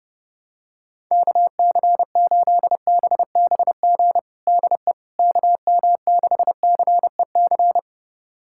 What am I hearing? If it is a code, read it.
Morse code, heard as a keyed tone, about 32 wpm, 700 Hz. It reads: KC8BBG DE KM6CEC
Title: USA Callsign DE Callsign 30wpm